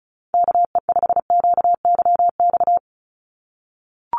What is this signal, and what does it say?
Morse code, heard as a keyed tone, about 34 wpm, 700 Hz. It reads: KE5QYX
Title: USA Call Signs 35wpm